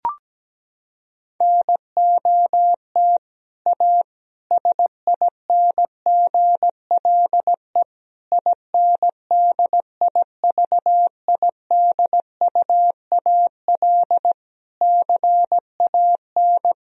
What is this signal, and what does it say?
Morse code, heard as a keyed tone, about 17 wpm, 700 Hz. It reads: NOT A SINGLE INDIVIDUAL CAN
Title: Sentences from Top 1000 Words - Review Part 4 - 17wpm